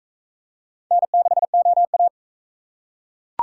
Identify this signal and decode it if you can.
Morse code, heard as a keyed tone, about 37 wpm, 700 Hz. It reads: N6OA